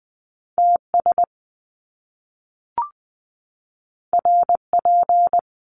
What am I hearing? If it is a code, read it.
Morse code, heard as a keyed tone, about 20 wpm, 700 Hz. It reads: TS RP
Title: Top Two Letter Combinations in Words 20wpm